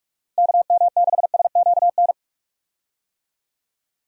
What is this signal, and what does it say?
Morse code, heard as a keyed tone, about 46 wpm, 700 Hz. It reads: KM6SQN
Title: USA Call Signs - 6 Characters - 45wpm